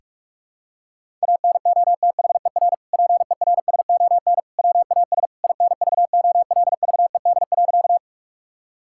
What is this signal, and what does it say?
Morse code, heard as a keyed tone, about 46 wpm, 700 Hz. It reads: ANOTHER PERSON WAS INVOLVED.
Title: Sentences from Top 1000 Words - Review Part 5 - 45wpm